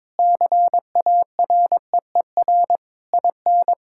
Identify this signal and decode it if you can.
Morse code, heard as a keyed tone, about 22 wpm, 700 Hz. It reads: CAREER IN
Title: Sentences from Top 1000 Words - Review Part 6 - 22wpm